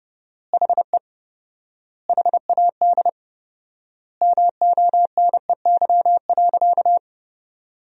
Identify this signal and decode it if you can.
Morse code, heard as a keyed tone, about 30 wpm, 700 Hz. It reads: HE HAD MONEY.
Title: 4x Word-Spacing - Sentences from Top 500 Words - 30wpm